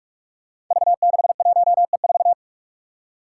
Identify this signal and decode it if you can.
Morse code, heard as a keyed tone, about 46 wpm, 700 Hz. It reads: V61E4
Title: Five Challenging Letter-Number Combinations 45wpm